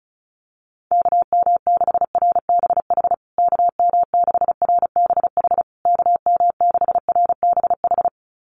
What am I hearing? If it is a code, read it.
Morse code, heard as a keyed tone, about 34 wpm, 700 Hz. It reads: KM6RBH KM6RBH KM6RBH
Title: USA Call Signs Repeated 3x 35wpm